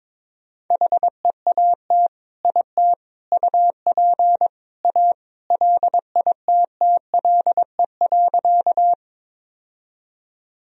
Morse code, heard as a keyed tone, about 22 wpm, 700 Hz. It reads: HEAT IT UP A LITTLE.
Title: Sentences from Top 1000 Words - Review Part 3 - 22wpm